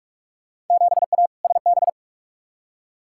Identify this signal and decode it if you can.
Morse code, heard as a keyed tone, about 43 wpm, 700 Hz. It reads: 7A SB
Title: Field Day Exchanges Encore - 45wpm